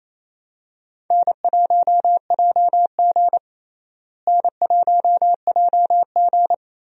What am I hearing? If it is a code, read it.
Morse code, heard as a keyed tone, about 28 wpm, 700 Hz. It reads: N1JG N1JG
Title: USA Call Signs - 4 Characters - Repeated 3x - 28wpm